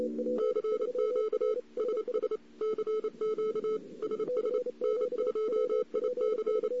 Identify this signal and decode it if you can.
Morse code, heard as a keyed tone, about 28 wpm, 430 Hz. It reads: CQ SS CQ SS N2IC
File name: CW / 2008-ss-cw-k5tr-op-n5rz -02:00